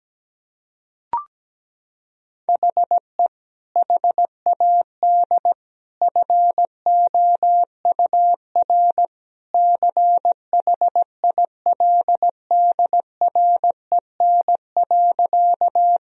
Morse code, heard as a keyed tone, about 17 wpm, 700 Hz. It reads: HE HAD FOUR CHILDREN.
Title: Sentences from Top 1000 Words - Review Part 1 - 17wpm